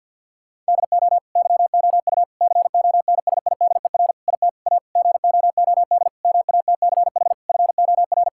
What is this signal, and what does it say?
Morse code, heard as a keyed tone, about 50 wpm, 700 Hz. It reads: DO YOU CONSIDER IT A GOOD MATCH FOR
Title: Sentences from Top 1600 Words 50wpm